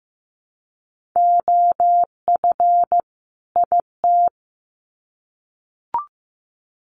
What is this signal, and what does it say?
Morse code, heard as a keyed tone, about 15 wpm, 700 Hz. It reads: OF IT
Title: Sets of 2 Words 15wpm